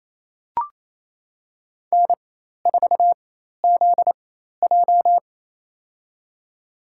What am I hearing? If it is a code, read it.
Morse code, heard as a keyed tone, about 28 wpm, spaced as for 13 wpm, 700 Hz. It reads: N4ZJ
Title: USA Call Signs - 4 Characters - 4x Character Spacing - 28wpm